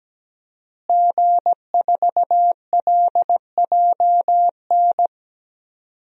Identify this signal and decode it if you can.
Morse code, heard as a keyed tone, about 17 wpm, 700 Hz. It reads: G4LJN
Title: DXCC-100 Call Signs - 5 Characters - Repeated 2x - 17wpm